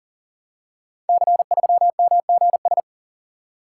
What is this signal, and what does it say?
Morse code, heard as a keyed tone, about 38 wpm, 700 Hz. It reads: C3MGS
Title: Code Groups - 5 Letter:Numbers - 40wpm